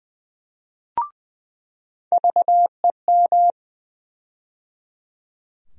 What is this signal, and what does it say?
Morse code, heard as a keyed tone, about 20 wpm, 700 Hz. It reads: VEM
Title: USA Call Sign Postfixes - 3 Letters 20wpm